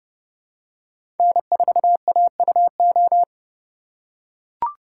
Morse code, heard as a keyed tone, about 29 wpm, 700 Hz. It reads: N4AUO
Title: USA Call Signs - 5 Characters - 8x Character Spacing - 30wpm